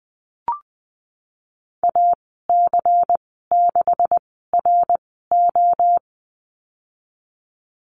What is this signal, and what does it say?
Morse code, heard as a keyed tone, about 20 wpm, spaced as for 15 wpm, 700 Hz. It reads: AC6RO
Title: 140 Lesson 23g - Letter f - Call Signs 20wpm